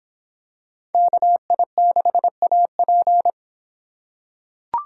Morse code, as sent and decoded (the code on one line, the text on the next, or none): -.- .. -.... .- .--.
KI6AP